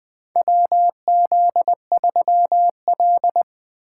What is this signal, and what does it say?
Morse code, heard as a keyed tone, about 20 wpm, 700 Hz. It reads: WZ3L
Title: USA Call Signs - 4 Characters - 2x Character Spacing - 20wpm